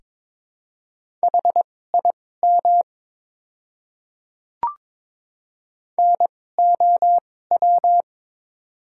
Morse code, heard as a keyed tone, about 22 wpm, spaced as for 16 wpm, 700 Hz. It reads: HIM NOW
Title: 2x Character Spacing - Top 500 Words - 22wpm